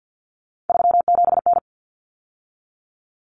Morse code, heard as a keyed tone, about 50 wpm, 700 Hz. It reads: 37N
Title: Common Cut Numbers - 3-Digit 50wpm